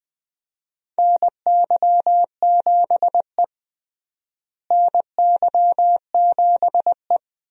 Morse code, heard as a keyed tone, about 20 wpm, 700 Hz. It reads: NY7E NY7E
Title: USA Call Signs - 4 Characters - Repeated 2x - 20wpm